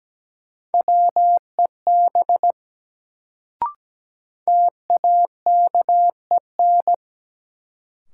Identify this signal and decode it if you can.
Morse code, heard as a keyed tone, about 17 wpm, 700 Hz. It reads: WEB TAKEN
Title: Sentences from Top 1000 Words - Review Part 4 - New Word Drill - 17wpm